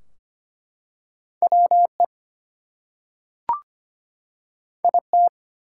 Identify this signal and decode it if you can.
Morse code, heard as a keyed tone, about 25 wpm, 700 Hz. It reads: WE IT
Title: Two Letter Words - 25wpm